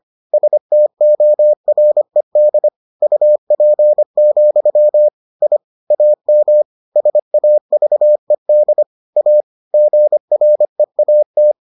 Morse code, heard as a keyed tone, about 25 wpm, 600 Hz. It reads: STORED UP, I AM SAVED A GREAT